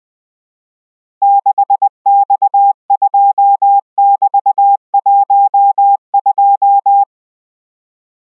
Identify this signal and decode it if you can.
Morse code, heard as a keyed tone, about 20 wpm, 800 Hz. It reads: 6X2<BT>12